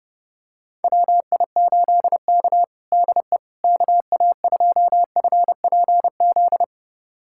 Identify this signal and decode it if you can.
Morse code, heard as a keyed tone, about 30 wpm, 700 Hz. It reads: WI8K DE KA2FPZ
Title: USA Callsign DE Callsign 30wpm